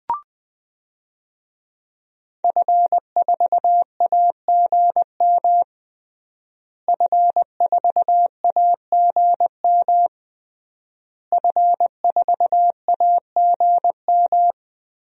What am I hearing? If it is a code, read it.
Morse code, heard as a keyed tone, about 20 wpm, 700 Hz. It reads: F4AGM F4AGM F4AGM
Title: DXCC-100 Call Signs - 5 Characters - Repeated 3x - 20wpm